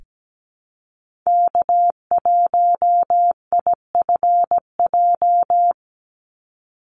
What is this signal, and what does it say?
Morse code, heard as a keyed tone, about 17 wpm, 700 Hz. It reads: K1IFJ
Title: Call Signs Repeated 3x - 17wpm